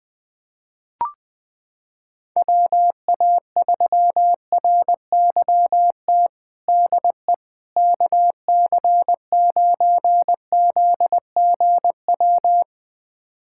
Morse code, heard as a keyed tone, about 21 wpm, 700 Hz. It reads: WA3RYT DE KC9ZGW
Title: USA Callsign DE Callsign 20wpm